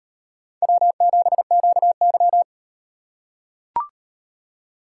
Morse code, heard as a keyed tone, about 38 wpm, 700 Hz. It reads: W7QY